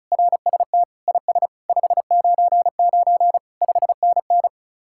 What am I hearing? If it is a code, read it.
Morse code, heard as a keyed tone, about 43 wpm, 700 Hz. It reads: RST IS 599 5NN
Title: Speed-Racing Common QSO Phrases - 35wpm